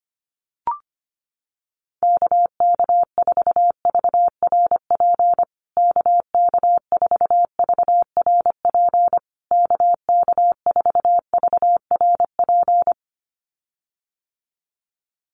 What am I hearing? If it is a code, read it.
Morse code, heard as a keyed tone, about 25 wpm, 700 Hz. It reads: KK4VRP KK4VRP KK4VRP
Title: USA Call Signs Repeated 3x 25wpm